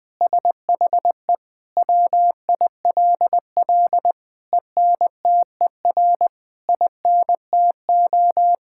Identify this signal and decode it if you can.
Morse code, heard as a keyed tone, about 20 wpm, 700 Hz. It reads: SHE WILL ENTER INTO
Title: Sentences from Top 1000 Words - Review Part 6 - 20wpm